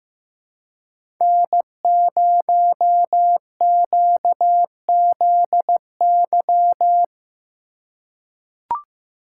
Morse code, heard as a keyed tone, about 15 wpm, 700 Hz. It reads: N0QZY
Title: USA Call Signs - 5 Characters - 4x Character Spacing - 15wpm